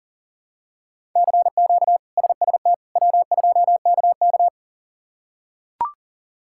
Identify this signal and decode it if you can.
Morse code, heard as a keyed tone, about 40 wpm, 700 Hz. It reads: CQ SST W2KK
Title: K1USN SST QSO Practice - 40wpm